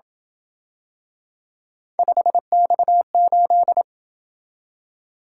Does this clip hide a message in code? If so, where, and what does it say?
Yes: Morse code from 2.0 to 3.8 s, heard as a keyed tone, about 27 wpm, 700 Hz. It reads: 5X8